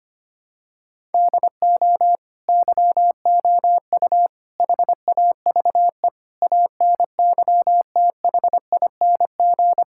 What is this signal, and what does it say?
Morse code, heard as a keyed tone, about 25 wpm, 700 Hz. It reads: DO YOU HAVE ANYTHING
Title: Sentences from Top 1000 Words - Review Part 5 - 25wpm